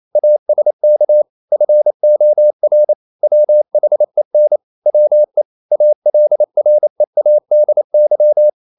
Morse code, heard as a keyed tone, about 28 wpm, 600 Hz. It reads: ASK FOR WHEN WE ALREADY
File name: CW / Practice / EnglishQuotations0228